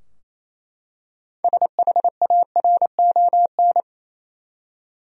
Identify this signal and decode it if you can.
Morse code, heard as a keyed tone, about 28 wpm, 700 Hz. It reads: SHARON
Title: USA Common Names - 28wpm